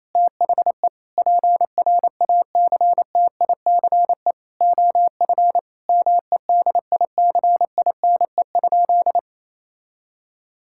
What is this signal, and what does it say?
Morse code, heard as a keyed tone, about 28 wpm, 700 Hz. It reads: THE PRACTICE OF MEDICINE?
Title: Sentences from Top 1000 Words - Review Part 6 - 28wpm